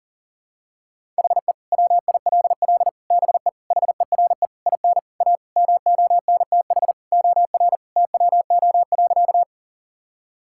Morse code, heard as a keyed tone, about 40 wpm, 700 Hz. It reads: HE WILL BE HERE IN A MONTH OR TWO.
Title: Sentences from Top 1000 Words - Review Part 5 - 40wpm